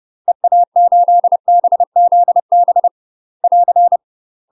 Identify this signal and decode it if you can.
Morse code, heard as a keyed tone, about 30 wpm, 700 Hz. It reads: EA8BZB <AR>
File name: UK MASTER QSOs TRAINING / 30WPM / UK MASTER QSOs 0023